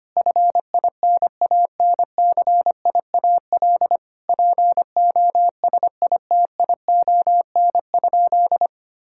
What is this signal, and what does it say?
Morse code, heard as a keyed tone, about 25 wpm, 700 Hz. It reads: FINANCIAL POSITION?
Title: Sentences from Top 1000 Words - Review Part 5 - 25wpm